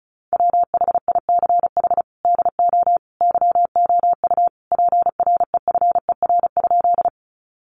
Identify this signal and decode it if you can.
Morse code, heard as a keyed tone, about 34 wpm, 700 Hz. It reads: WHICH DO YOU PREFER?
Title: Sentences from Top 1100 Words 35wpm